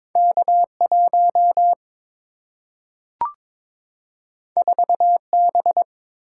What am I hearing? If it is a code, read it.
Morse code, heard as a keyed tone, about 22 wpm, 700 Hz. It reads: K 1 4 B
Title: Single-Letter Single-Number 22wpm